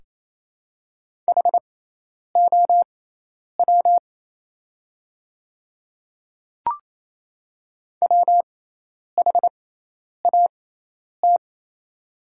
Morse code, heard as a keyed tone, about 28 wpm, spaced as for 10 wpm, 700 Hz. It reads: HOW WHAT
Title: 6x Character Spacing - Top 100 Words - 28wpm